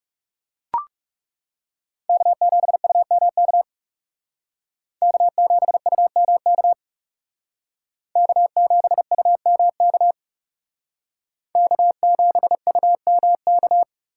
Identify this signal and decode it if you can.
Morse code, heard as a keyed tone, about 38 wpm, 700 Hz. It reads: K7UMK K7UMK K7UMK K7UMK